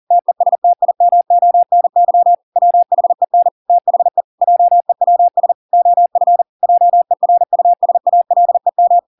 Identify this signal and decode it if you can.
Morse code, heard as a keyed tone, about 40 wpm, 700 Hz. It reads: TESTIMONY WHEN THE JEWS OF JERUSALEM